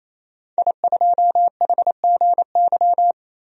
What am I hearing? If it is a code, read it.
Morse code, heard as a keyed tone, about 28 wpm, 700 Hz. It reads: I2HGY
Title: DXCC-100 Call Signs - 5 Characters - 28wpm